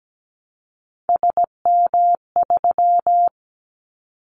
Morse code, heard as a keyed tone, about 17 wpm, 700 Hz. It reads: SM3
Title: Callsign Prefixes for Top 25 DXCC - 17wpm